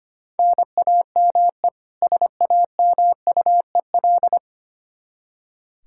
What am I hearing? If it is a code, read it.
Morse code, heard as a keyed tone, about 25 wpm, 700 Hz. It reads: NAME SAMUEL
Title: Practice QSO - Exchange 1 - Beginning - 25wpm